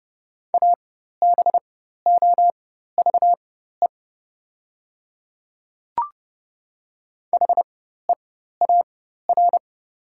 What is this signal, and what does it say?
Morse code, heard as a keyed tone, about 30 wpm, spaced as for 14 wpm, 700 Hz. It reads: ABOVE HEAR
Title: 4x Character Spacing - Top 500 Words - 30wpm